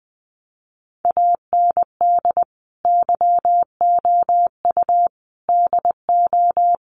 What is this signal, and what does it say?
Morse code, heard as a keyed tone, about 20 wpm, 700 Hz. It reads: AND YOU DO
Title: Sets of 5 Words Encore2 20wpm